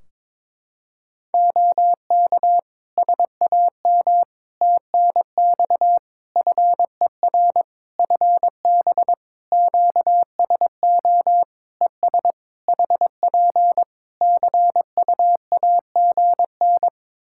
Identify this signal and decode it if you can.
Morse code, heard as a keyed tone, about 22 wpm, 700 Hz. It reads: OK SAM TNX FER FB QSO ES HP CUAGN
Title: Practice QSO - Exchange 4 - Ending - 22wpm